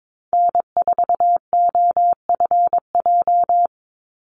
Morse code, heard as a keyed tone, about 22 wpm, 700 Hz. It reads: N4OFJ
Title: USA Call Signs 22wpm